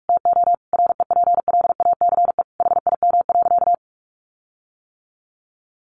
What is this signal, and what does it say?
Morse code, heard as a keyed tone, about 44 wpm, 700 Hz. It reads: TO REPLACE HIM.
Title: Sentences from Top 1400 Words 45wpm